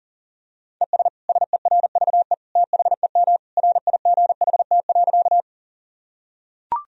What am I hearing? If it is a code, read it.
Morse code, heard as a keyed tone, about 40 wpm, 700 Hz. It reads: ES SERVE THEM RIGHT.
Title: Sentences from Top 400 Words and Abbreviations 40wpm